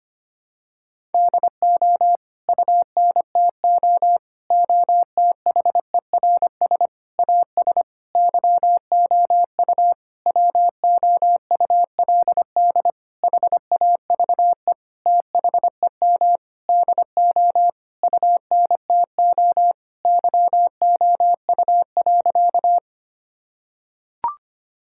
Morse code, heard as a keyed tone, about 25 wpm, 700 Hz. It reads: DO UNTO OTHERS AS YOU WOULD HAVE THEM DO UNTO YOU.
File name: practice / 25wpm / Sentences with 9-Word Phrases - 25wpm